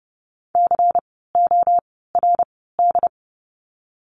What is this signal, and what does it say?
Morse code, heard as a keyed tone, about 30 wpm, 700 Hz. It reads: C O R D
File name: Lesson 16f - Letter c - Words 30wpm